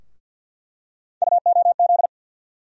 Morse code, heard as a keyed tone, about 46 wpm, 700 Hz. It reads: UOZ
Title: USA Call Sign Postfixes - 3 Letters 50wpm